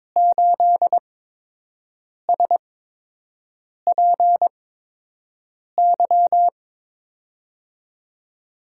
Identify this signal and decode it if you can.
Morse code, heard as a keyed tone, about 22 wpm, spaced as for 6 wpm, 700 Hz. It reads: 8SPY
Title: USA Call Signs - 5 Characters - 8x Character Spacing - 22wpm